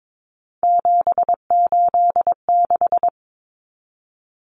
Morse code, heard as a keyed tone, about 22 wpm, 700 Hz. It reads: 786
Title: Common Cut Numbers - 3-Digit 22wpm